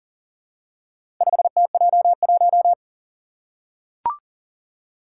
Morse code, heard as a keyed tone, about 40 wpm, 700 Hz. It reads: 5TJ1
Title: Four Challenging Letter-Number Combinations 40wpm